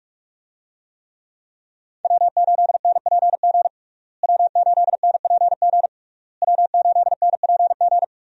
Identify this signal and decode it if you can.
Morse code, heard as a keyed tone, about 46 wpm, 700 Hz. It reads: W8NPG W8NPG W8NPG
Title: USA Call Signs - 5 Characters - Repeated 3x - 45wpm